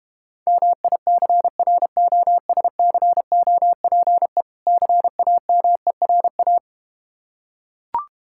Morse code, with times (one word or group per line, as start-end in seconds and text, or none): 0.5-4.4 MICROSCOPE
4.7-6.6 CAMERA